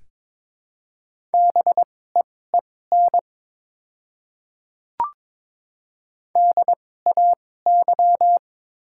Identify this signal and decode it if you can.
Morse code, heard as a keyed tone, about 23 wpm, spaced as for 16 wpm, 690 Hz. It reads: BEEN DAY
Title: 2x Character Spacing - Top 100 Words - 22wpm